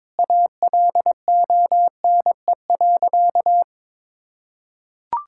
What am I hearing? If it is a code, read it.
Morse code, heard as a keyed tone, about 22 wpm, 700 Hz. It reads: ALONE.
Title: Sentences from Top 1000 Words - Review Part 6 - 22wpm